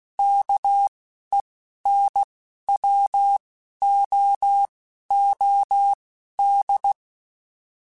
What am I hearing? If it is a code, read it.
Morse code, heard as a keyed tone, about 16 wpm, spaced as for 11 wpm, 800 Hz. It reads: KENWOOD
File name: QSOs / QSOs-mp3s / kc7haa@10